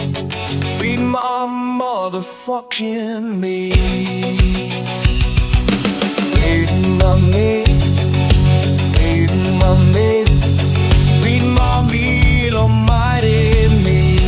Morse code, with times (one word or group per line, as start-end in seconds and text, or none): none